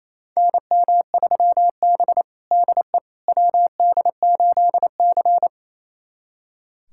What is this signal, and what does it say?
Morse code, heard as a keyed tone, about 28 wpm, 700 Hz. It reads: NM3B DE WD8C
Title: USA Callsign DE Callsign - 4 Characters - 28wpm